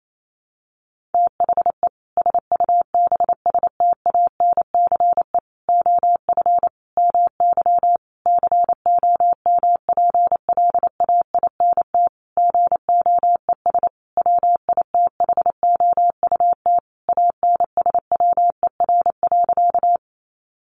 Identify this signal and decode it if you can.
Morse code, heard as a keyed tone, about 28 wpm, 700 Hz. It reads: THE SUBSTANCE OF MY COMPLAINT GOES WITHOUT ANSWER.
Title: Sentences from Top 1500 Words 28wpm